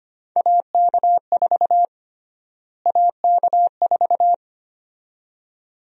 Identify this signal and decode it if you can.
Morse code, heard as a keyed tone, about 25 wpm, 700 Hz. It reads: AK4 AK4
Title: USA Call Sign Prefixes - Repeated 2x - 25wpm